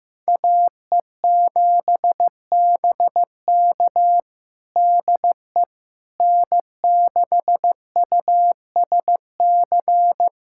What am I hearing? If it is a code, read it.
Morse code, heard as a keyed tone, about 15 wpm, 700 Hz. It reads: AE7BK DE N6USC
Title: USA Callsign DE Callsign - 5 Characters - 15wpm